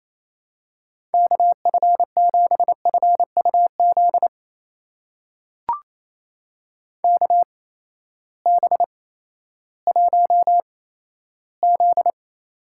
Morse code, heard as a keyed tone, about 28 wpm, spaced as for 14 wpm, 700 Hz. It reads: KF7FUZ KB1Z
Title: USA Call Signs - 6 Characters - 8x Character Spacing - 28wpm